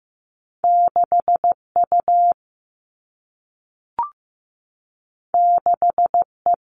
Morse code, heard as a keyed tone, about 15 wpm, 700 Hz. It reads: 6U 6E
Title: All Cut Numbers - 2-Digit 15wpm